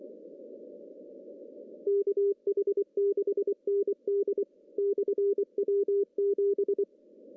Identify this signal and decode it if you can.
Morse code, heard as a keyed tone, about 24 wpm, 400 Hz. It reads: KH6ND/W7
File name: CW / cqww-cw-k5tr